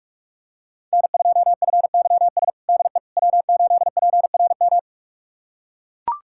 Morse code, heard as a keyed tone, about 42 wpm, 700 Hz. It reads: N2FYS DE W8PRM
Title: USA Callsign DE Callsign - 5 Characters - 45wpm